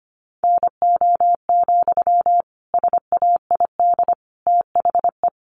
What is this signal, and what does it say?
Morse code, heard as a keyed tone, about 25 wpm, 700 Hz. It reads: NO, SAID THE
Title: Sentences from Top 300 Words 25wpm